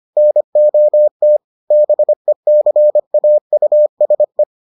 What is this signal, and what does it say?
Morse code, heard as a keyed tone, about 25 wpm, 600 Hz. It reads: NOT BECAUSE